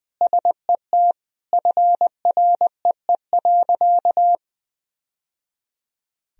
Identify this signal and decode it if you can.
Morse code, heard as a keyed tone, about 20 wpm, 700 Hz. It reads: SET FREE.
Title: Sentences from Top 1000 Words - Review Part 1 - 20wpm